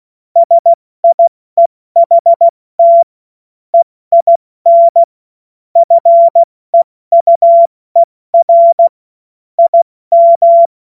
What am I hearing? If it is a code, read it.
Morse code, heard as a keyed tone, about 16 wpm, 680 Hz. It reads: SIEHT EIN FEUER IM
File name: Morsekurs / Witze / Witz 14. Juni 2022 70 F80